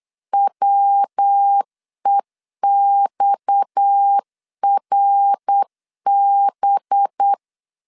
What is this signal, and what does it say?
Morse code, heard as a keyed tone, about 8 wpm, 800 Hz. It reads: WEXRB
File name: CW / DET / 1 6 1500C S B 8wpm 5142